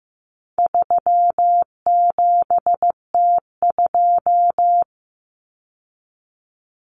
Morse code, heard as a keyed tone, about 15 wpm, 700 Hz. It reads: 37T2
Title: Common Cut Numbers - 4-Digit 15wpm